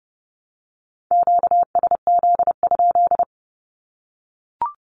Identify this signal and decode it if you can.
Morse code, heard as a keyed tone, about 30 wpm, 700 Hz. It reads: QSZ?
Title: Q-Codes Repeated 3x 30wpm